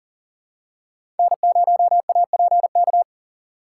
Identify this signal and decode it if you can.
Morse code, heard as a keyed tone, about 40 wpm, 700 Hz. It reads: N0APK
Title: USA Call Signs - 5 Characters - 4x Character Spacing - 40wpm